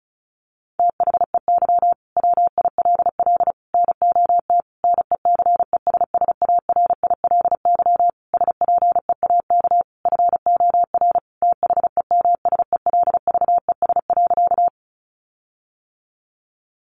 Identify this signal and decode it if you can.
Morse code, heard as a keyed tone, about 34 wpm, 700 Hz. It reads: THEY WILL NOT NECESSARILY SPEAK FOR THEMSELVES.
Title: Sentences from Top 1400 Words 35wpm